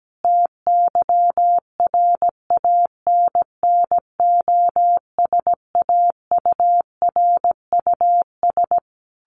Morse code, heard as a keyed tone, about 17 wpm, 700 Hz. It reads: TYRANNOSAURUS
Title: A Sound Of Thunder 17wpm